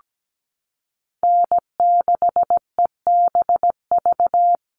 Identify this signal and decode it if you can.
Morse code, heard as a keyed tone, about 17 wpm, 700 Hz. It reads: N6EBV
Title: USA Call Signs Repeated 3x 17wpm